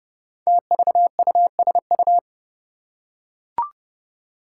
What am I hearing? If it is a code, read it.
Morse code, heard as a keyed tone, about 32 wpm, 700 Hz. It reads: TVUSU
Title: Five Challenging Letter-Number Combinations 30wpm